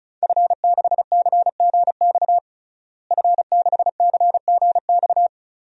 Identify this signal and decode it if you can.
Morse code, heard as a keyed tone, about 36 wpm, 690 Hz. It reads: F6CGX F6CGX
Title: DXCC-100 Call Signs - 5 Characters - Repeated 3x - 35wpm